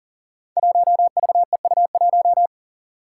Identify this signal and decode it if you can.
Morse code, heard as a keyed tone, about 40 wpm, 700 Hz. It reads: 1VEU1
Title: Five Challenging Letter-Number Combinations 40wpm